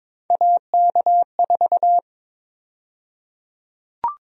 Morse code, heard as a keyed tone, about 22 wpm, 700 Hz. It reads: AK4
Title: USA Call Sign Prefixes - Repeated 2x - 22wpm